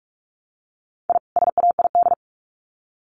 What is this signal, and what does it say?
Morse code, heard as a keyed tone, about 44 wpm, 700 Hz. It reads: I SAID
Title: Sets of 2 Words 45wpm